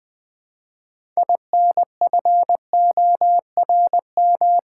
Morse code, heard as a keyed tone, about 20 wpm, 700 Hz. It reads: INFORM
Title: Build A Word - 3 Letters At A Time - 6-Letter Words - 20wpm